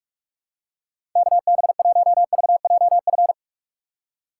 Morse code, heard as a keyed tone, about 46 wpm, 700 Hz. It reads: KB1VJF
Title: USA Call Signs - 6 Characters - 6x Character Spacing - 45wpm